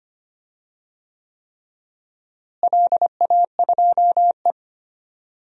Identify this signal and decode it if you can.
Morse code, heard as a keyed tone, about 25 wpm, 700 Hz. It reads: LA2E
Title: DXCC-100 Call Signs - 4 Characters - Repeated 3x - 25wpm